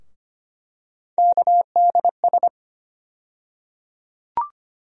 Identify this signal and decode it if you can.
Morse code, heard as a keyed tone, about 25 wpm, 700 Hz. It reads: KDS
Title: USA Call Sign Postfixes - 3 Letters 25wpm